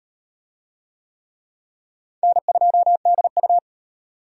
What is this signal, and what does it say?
Morse code, heard as a keyed tone, about 37 wpm, 700 Hz. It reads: N2DU